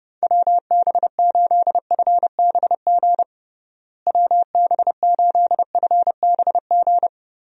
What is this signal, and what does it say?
Morse code, heard as a keyed tone, about 32 wpm, 700 Hz. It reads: WB8FBG WB8FBG
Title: USA Call Signs - 6 Characters - Repeated 2x - 30wpm